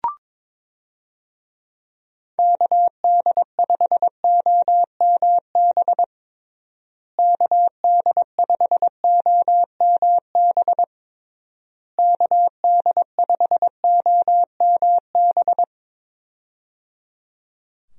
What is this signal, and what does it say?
Morse code, heard as a keyed tone, about 22 wpm, 700 Hz. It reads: KD5OMB KD5OMB KD5OMB
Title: USA Call Signs - 6 Characters - Repeated 3x - 22wpm